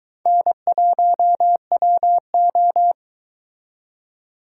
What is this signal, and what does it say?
Morse code, heard as a keyed tone, about 22 wpm, 700 Hz. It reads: N1WO